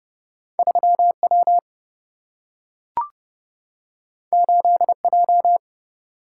Morse code, heard as a keyed tone, about 29 wpm, 700 Hz. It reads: 3 W 8 J
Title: Single-Letter Single-Number 30wpm